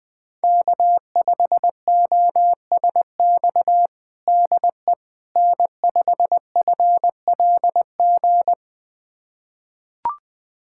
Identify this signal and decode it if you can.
Morse code, heard as a keyed tone, about 20 wpm, 700 Hz. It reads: K5OSX DE N5FLG
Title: USA Callsign DE Callsign - 5 Characters - 20wpm